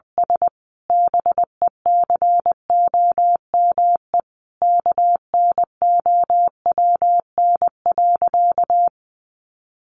Morse code, heard as a keyed tone, about 20 wpm, 700 Hz. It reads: S BECOME KNOWN.
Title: Sentences from Top 1300 Words 20wpm